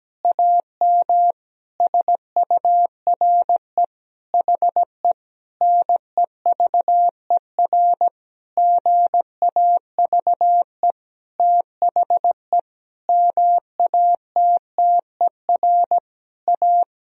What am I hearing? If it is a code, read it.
Morse code, heard as a keyed tone, about 17 wpm, 700 Hz. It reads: AM SURE HE NEVER GAVE THE MATTER A
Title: Sentences from Top 1000 Words - Review Part 2 - 17wpm